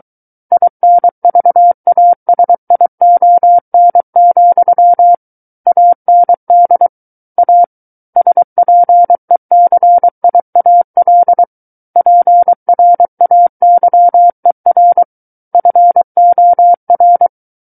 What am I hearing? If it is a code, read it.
Morse code, heard as a keyed tone, about 23 wpm, 700 Hz. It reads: INVASION, AND A SPECIAL PRAYER FOR
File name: Book One / 14-Chapter 14 - 23 WPM